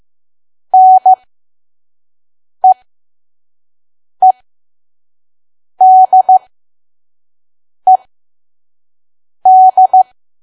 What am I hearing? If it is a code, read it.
Morse code, heard as a keyed tone, about 15 wpm, spaced as for 5 wpm, 750 Hz. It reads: NEEDED
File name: Archive / 5 WPM files / PM